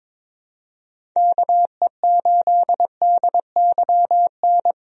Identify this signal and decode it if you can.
Morse code, heard as a keyed tone, about 22 wpm, 700 Hz. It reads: KE8DYN
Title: USA Call Signs - 6 Characters - 22wpm